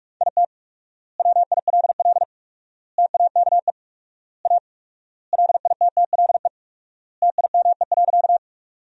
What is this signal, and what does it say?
Morse code, heard as a keyed tone, about 47 wpm, 700 Hz. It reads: IT WILL TAKE A LITTLE TIME.
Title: 4x Word-Spacing - Sentences from Top 200 Words - 45wpm